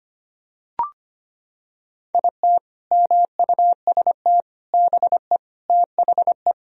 Morse code, heard as a keyed tone, about 25 wpm, 700 Hz. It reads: IT MUST BE THE
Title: Sentences from Top 1000 Words - Review Part 4 - 25wpm